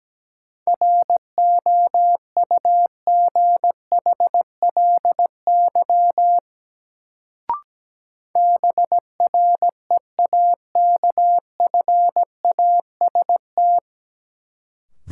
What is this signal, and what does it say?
Morse code, heard as a keyed tone, about 17 wpm, 700 Hz. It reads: ROUGHLY BREAKFAST
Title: Sentences from Top 1500 Words - New Word Drill - 17wpm